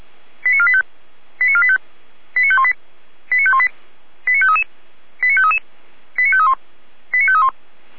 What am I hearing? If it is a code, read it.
Five-tone selective call (ZVEI), heard as quick runs of five tones, tones of about 70 ms. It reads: 89485 89485 89518 89518 89533 89533 89531 89531